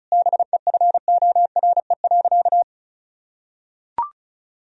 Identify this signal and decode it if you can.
Morse code, heard as a keyed tone, about 34 wpm, 700 Hz. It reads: BEFORE.
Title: Sentences from Top 1000 Words - Review Part 1 - 35wpm